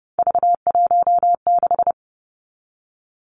Morse code, heard as a keyed tone, about 30 wpm, 700 Hz. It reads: V16
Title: Three Challenging Letter-Number Combinations 30wpm